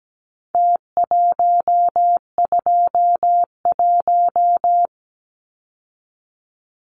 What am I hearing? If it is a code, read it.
Morse code, heard as a keyed tone, about 17 wpm, 700 Hz. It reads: T121
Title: Numbers - Contest Style - 4-Digit 17wpm